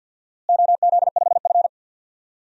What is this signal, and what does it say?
Morse code, heard as a keyed tone, about 47 wpm, 700 Hz. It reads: OZ5F
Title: DXCC-100 Call Signs - 4 Characters - Repeated 3x - 50wpm